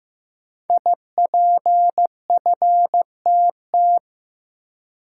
Morse code, heard as a keyed tone, about 15 wpm, 700 Hz. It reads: IPFTT
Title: Code Groups - 5 Letters - 15wpm